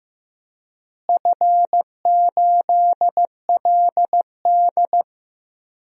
Morse code, heard as a keyed tone, about 15 wpm, 700 Hz. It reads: F8LD
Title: DXCC-100 Call Signs - 4 Characters - Repeated 2x - 15wpm